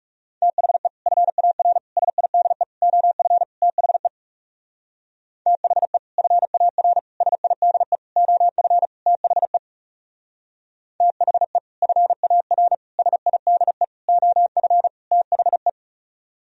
Morse code, heard as a keyed tone, about 35 wpm, 700 Hz. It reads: THE FAR SIDE OF THE THE FAR SIDE OF THE THE FAR SIDE OF THE
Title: Speed-Racing - Sets of 5 Words 30wpm